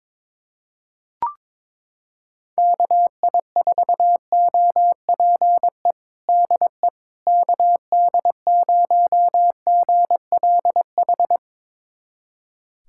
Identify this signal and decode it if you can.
Morse code, heard as a keyed tone, about 22 wpm, 700 Hz. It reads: KI4OPE DE KD0GLH
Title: USA Callsign DE Callsign 22wpm